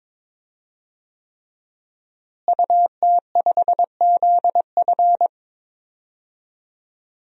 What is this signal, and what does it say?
Morse code, heard as a keyed tone, about 21 wpm, 700 Hz. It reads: UT5ZF
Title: DXCC-100 Call Signs - 5 Characters - Repeated 3x - 22wpm